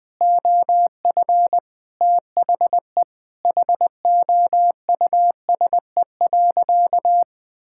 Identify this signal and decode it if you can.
Morse code, heard as a keyed tone, about 20 wpm, 700 Hz. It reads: OF THE HOUSE.
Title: Sentences from Top 1600 Words 20wpm